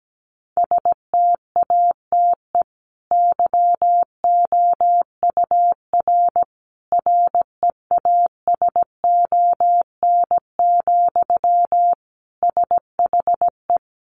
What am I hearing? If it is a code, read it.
Morse code, heard as a keyed tone, about 17 wpm, 700 Hz. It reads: STATE YOUR REASON, SHE
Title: Sentences from Top 700 Words 17wpm